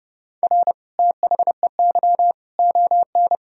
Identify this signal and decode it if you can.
Morse code, heard as a keyed tone, about 30 wpm, 700 Hz. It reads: R THEY ON
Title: Sentences from Top 500 Words and Abbreviations 30wpm